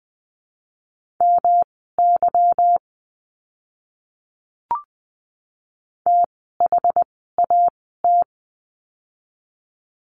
Morse code, heard as a keyed tone, about 20 wpm, spaced as for 14 wpm, 700 Hz. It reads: MY THAT
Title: 244 Lesson 39f - Letter z - Words 20wpm